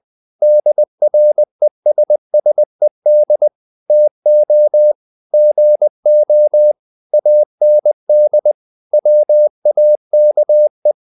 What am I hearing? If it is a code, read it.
Morse code, heard as a keyed tone, about 20 wpm, 600 Hz. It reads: DRESSED TO GO AND WAKE
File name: the old man and the sea / the old man and the sea part 1 20wpm